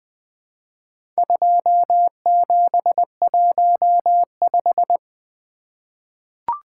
Morse code, heard as a keyed tone, about 21 wpm, 700 Hz. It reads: 2715
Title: Four Letter-Number Combinations - 20wpm